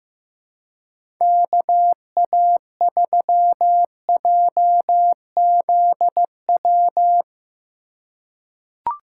Morse code, heard as a keyed tone, about 15 wpm, 700 Hz. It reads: KA3JZW
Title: USA Call Signs - 6 Characters - 3x Character Spacing - 15wpm